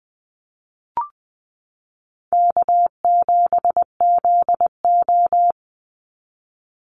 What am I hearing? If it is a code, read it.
Morse code, heard as a keyed tone, about 20 wpm, 700 Hz. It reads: K7ZO
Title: NA SOTA Chaser Callsigns 20wpm